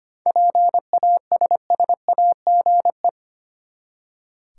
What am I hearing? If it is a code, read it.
Morse code, heard as a keyed tone, about 25 wpm, 700 Hz. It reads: PASSAGE
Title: Seven Letter Words - 25wpm